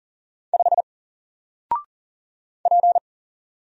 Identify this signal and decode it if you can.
Morse code, heard as a keyed tone, about 40 wpm, 700 Hz. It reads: 5 P
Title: Single Character (Letter:Number:Punctuation:Prosign) Alt 40wpm